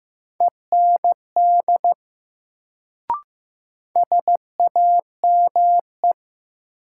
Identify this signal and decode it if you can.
Morse code, heard as a keyed tone, about 15 wpm, 700 Hz. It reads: END SAME
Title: Sentences from Top 200 Words - New Word Drill - 15wpm